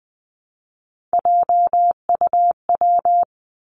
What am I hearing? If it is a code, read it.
Morse code, heard as a keyed tone, about 20 wpm, 700 Hz. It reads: JUW
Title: Three Challenging Letter-Number Combinations 20wpm